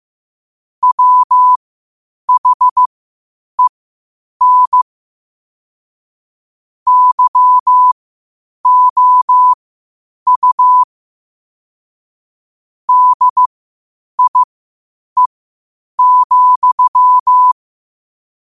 Morse code, heard as a keyed tone, about 15 wpm, spaced as for 8 wpm, 1000 Hz. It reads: WHEN YOU DIE,
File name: qotdfiles / 8wpm